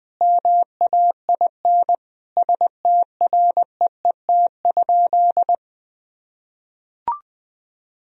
Morse code, heard as a keyed tone, about 21 wpm, 700 Hz. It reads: MAIN STREET?
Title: Sentences from Top 1000 Words - Review Part 2 - 20wpm